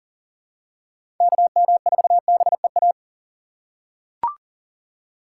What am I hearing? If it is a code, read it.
Morse code, heard as a keyed tone, about 40 wpm, 700 Hz. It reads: KM4BEA
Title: USA Call Signs - 3x Character Spacing - 40wpm